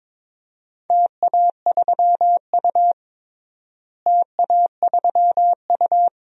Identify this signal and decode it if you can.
Morse code, heard as a keyed tone, about 22 wpm, 700 Hz. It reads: TA3U TA3U
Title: DXCC-100 Call Signs - 4 Characters - Repeated 2x - 22wpm